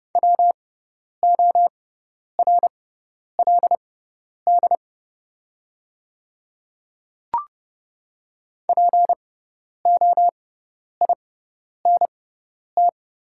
Morse code, heard as a keyed tone, about 30 wpm, spaced as for 10 wpm, 700 Hz. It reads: WORLD POINT
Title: 6x Character Spacing - Top 200 Words - 30wpm